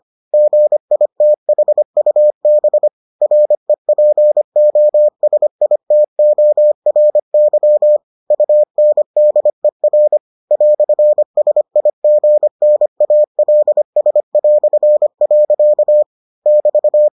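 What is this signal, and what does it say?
Morse code, heard as a keyed tone, about 25 wpm, 600 Hz. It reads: GITHUB REPOSITORY UNDER "SIGNALS". <BT>
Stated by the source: Morse edition of the Foundations of Amateur Radio episodes.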